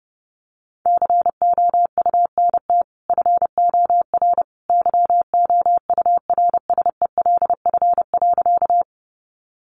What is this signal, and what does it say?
Morse code, heard as a keyed tone, about 30 wpm, 700 Hz. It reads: COUNT FOR YOURSELF.
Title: Sentences from Top 1500 Words 30wpm